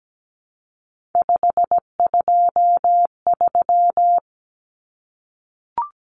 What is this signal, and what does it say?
Morse code, heard as a keyed tone, about 17 wpm, 700 Hz. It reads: 523
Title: Numbers - 3-Digit 17wpm